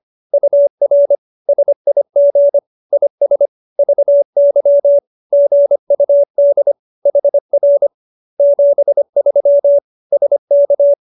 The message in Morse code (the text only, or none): UR SIG IS VY GUD HR 73 SK